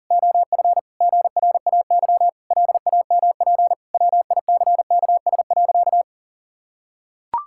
Morse code, heard as a keyed tone, about 40 wpm, 700 Hz. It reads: OF GRAY LAMP WICKS.
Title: Pangram Challenge v2 - 40wpm